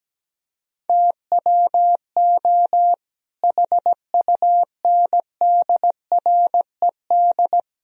Morse code, heard as a keyed tone, about 17 wpm, 700 Hz. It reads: TWO HUNDRED
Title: Sentences from Top 1000 Words - Review Part 2 - 17wpm